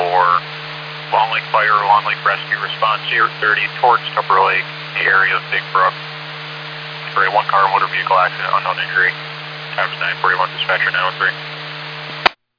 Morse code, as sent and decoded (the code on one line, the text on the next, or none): none